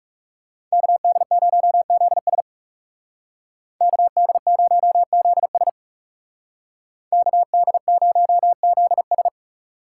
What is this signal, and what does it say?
Morse code, heard as a keyed tone, about 37 wpm, 700 Hz. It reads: KD0ZS KD0ZS KD0ZS
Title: Speed Racing - USA 2x2 Call Signs 30wpm